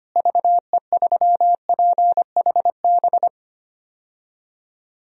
Morse code, heard as a keyed tone, about 25 wpm, 700 Hz. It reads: VE3PHB
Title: DXCC-100 Call Signs - 6 Characters - Repeated 3x - 25wpm